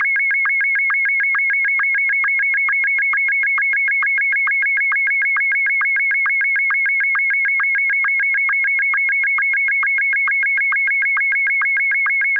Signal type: SSTV (slow-scan television)